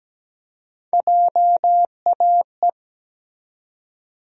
Morse code, heard as a keyed tone, about 17 wpm, 700 Hz. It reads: JAE
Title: USA Call Sign Postfixes - 3 Letters 17wpm